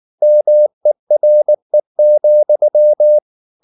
Morse code, heard as a keyed tone, about 19 wpm, 600 Hz. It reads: MERE,
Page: Ferie 19 wpm